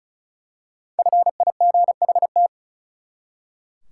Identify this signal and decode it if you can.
Morse code, heard as a keyed tone, about 34 wpm, 700 Hz. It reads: FIGHT
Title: Five Letter Words - 35wpm